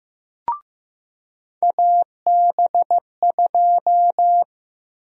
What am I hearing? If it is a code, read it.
Morse code, heard as a keyed tone, about 15 wpm, 700 Hz. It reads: AB2
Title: USA Call Sign Prefixes - 15wpm